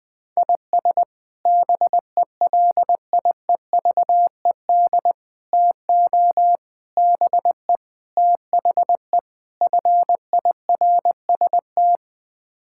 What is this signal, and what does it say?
Morse code, heard as a keyed tone, about 20 wpm, 700 Hz. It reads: IS BELIEVED TO BE THE FIRST
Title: Sets of 6 Words - 20wpm